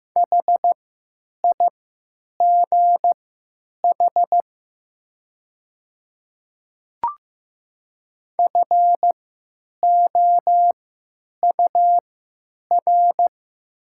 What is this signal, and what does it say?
Morse code, heard as a keyed tone, about 15 wpm, spaced as for 8 wpm, 700 Hz. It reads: HIGH FOUR
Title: 3x Character Spacing - Top 500 Words - 15wpm